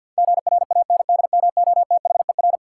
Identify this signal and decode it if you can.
Morse code, heard as a keyed tone, about 49 wpm, 700 Hz. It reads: GRANDMOTHER
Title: Sentences from Top 1200 Words - New Word Drill - 50wpm